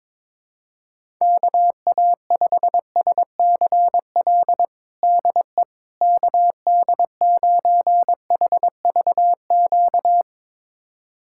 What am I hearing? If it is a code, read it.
Morse code, heard as a keyed tone, about 22 wpm, 700 Hz. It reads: KA5SCL DE KD9HVQ
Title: USA Callsign DE Callsign - 6 Characters - 22wpm